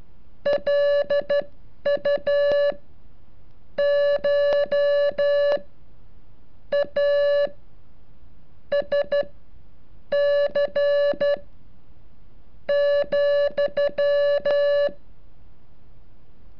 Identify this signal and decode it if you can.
Morse code, heard as a keyed tone, about 13 wpm, spaced as for 6 wpm, 570 Hz. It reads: LUCHASC,
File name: curso / ejercicio 36